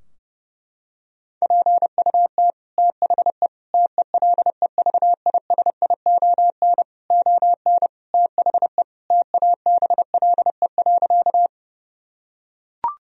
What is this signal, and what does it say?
Morse code, heard as a keyed tone, about 30 wpm, 700 Hz. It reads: PUT THE TELEVISION ON THE TABLE.
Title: Sentences from Top 1000 Words - Review Part 4 - 30wpm